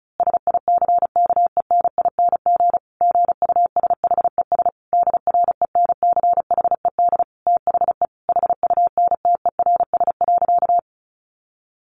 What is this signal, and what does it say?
Morse code, heard as a keyed tone, about 34 wpm, 700 Hz. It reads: SICKENING GUSHES DRENCHED THE HUNTERS.
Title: A Sound Of Thunder 35wpm